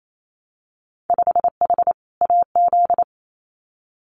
Morse code, heard as a keyed tone, about 28 wpm, 700 Hz. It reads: 5H AZ
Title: Winter Field Day Exchanges Encore 28wpm